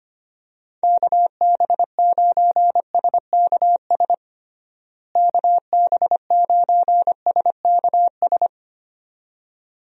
Morse code, heard as a keyed tone, about 25 wpm, 700 Hz. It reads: KB9SKS KB9SKS
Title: USA Call Signs - 6 Characters - Repeated 2x - 25wpm